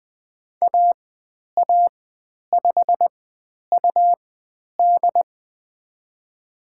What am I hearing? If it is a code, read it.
Morse code, heard as a keyed tone, about 20 wpm, spaced as for 10 wpm, 700 Hz. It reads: AA5UD
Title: 102 Lesson 17g - Letter u - Call Signs 20wpm